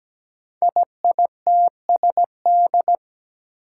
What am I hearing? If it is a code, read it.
Morse code, heard as a keyed tone, about 17 wpm, 700 Hz. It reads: IITSD
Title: Code Groups - 5 Letters - 17wpm